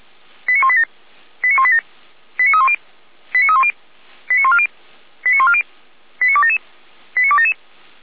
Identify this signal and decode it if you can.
Five-tone selective call (ZVEI), heard as quick runs of five tones, tones of about 70 ms. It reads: 89187 89187 89310 89310 89140 89140 89277 89277